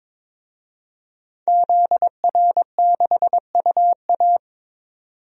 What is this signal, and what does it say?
Morse code, heard as a keyed tone, about 22 wpm, 700 Hz. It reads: ZR6UA
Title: DXCC-100 Call Signs - 5 Characters - 22wpm